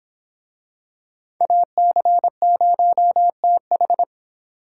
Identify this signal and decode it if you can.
Morse code, heard as a keyed tone, about 25 wpm, 700 Hz. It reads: AC0TH